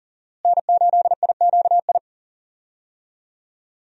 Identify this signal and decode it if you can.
Morse code, heard as a keyed tone, about 37 wpm, 700 Hz. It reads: N8IQI
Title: USA Call Signs - 5 Characters - Repeated 2x - 40wpm